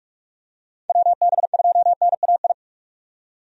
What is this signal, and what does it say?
Morse code, heard as a keyed tone, about 46 wpm, 700 Hz. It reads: WB2NAI
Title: USA Call Signs - 6 Characters - 45wpm